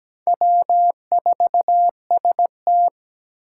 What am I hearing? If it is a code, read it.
Morse code, heard as a keyed tone, about 17 wpm, 700 Hz. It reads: W4ST
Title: Four Challenging Letter-Number Combinations 17wpm